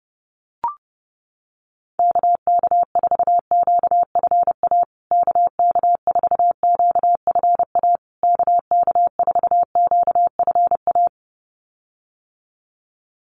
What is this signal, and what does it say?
Morse code, heard as a keyed tone, about 30 wpm, 700 Hz. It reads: KK4QFA KK4QFA KK4QFA
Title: USA Call Signs Repeated 3x 30wpm